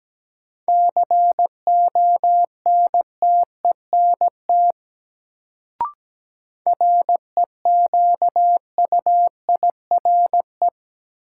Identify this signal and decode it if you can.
Morse code, heard as a keyed tone, about 17 wpm, 700 Hz. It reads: CONTENT REQUIRE
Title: Sentences from Top 900 Words - New Word Drill - 17wpm